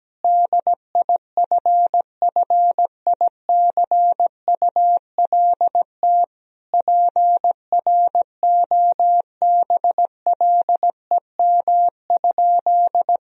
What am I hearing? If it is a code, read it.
Morse code, heard as a keyed tone, about 17 wpm, 700 Hz. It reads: DIFFICULT PROBLEM?
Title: Sentences from Top 1000 Words - Review Part 2 - 17wpm